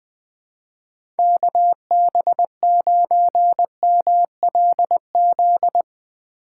Morse code, heard as a keyed tone, about 20 wpm, 700 Hz. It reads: KB9MLZ
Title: USA Call Signs - 6 Characters - 3x Character Spacing - 20wpm